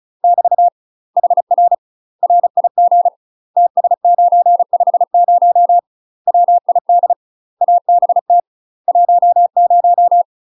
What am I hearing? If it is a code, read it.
Morse code, heard as a keyed tone, about 35 wpm, 700 Hz. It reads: <BT> HR RIG TS950 WID ABT 10
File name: UK MASTER QSOs TRAINING / 35WPM / UK MASTER QSOs 0039